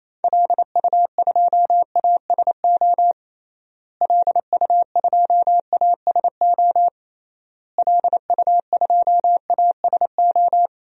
Morse code, heard as a keyed tone, about 28 wpm, 700 Hz. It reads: LU2ASO LU2ASO LU2ASO
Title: DXCC-100 Call Signs - 6 Characters - Repeated 3x - 28wpm